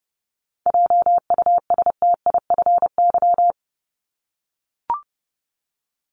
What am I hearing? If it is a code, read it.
Morse code, heard as a keyed tone, about 30 wpm, 700 Hz. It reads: JUSTIFY
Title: Top 1751-2000 Words 30wpm